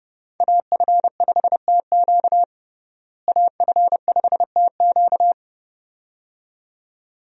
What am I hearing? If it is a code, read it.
Morse code, heard as a keyed tone, about 29 wpm, 700 Hz. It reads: AF5TQ AF5TQ
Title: USA Call Signs - 5 Characters - Repeated 2x - 30wpm